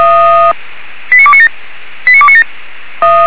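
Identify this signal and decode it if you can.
Five-tone selective call (ZVEI), heard as quick runs of five tones, tones of about 70 ms. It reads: 89297 89297
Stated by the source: ZVEI five-tone selective call received on SDR.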